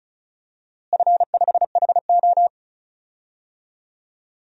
Morse code, heard as a keyed tone, about 36 wpm, 690 Hz. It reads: F5HO
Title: DXCC-100 Call Signs - 4 Characters - Repeated 3x - 35wpm